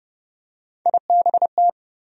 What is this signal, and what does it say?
Morse code, heard as a keyed tone, about 30 wpm, 700 Hz. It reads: IBT
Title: USA Call Sign Postfixes - 3 Letters 30wpm